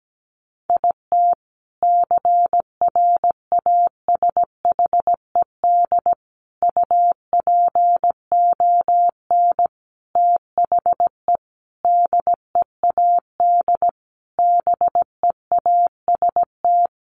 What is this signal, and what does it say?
Morse code, heard as a keyed tone, about 17 wpm, 700 Hz. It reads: IT CRASHED UPON THE DEAD BEAST
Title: A Sound Of Thunder 17wpm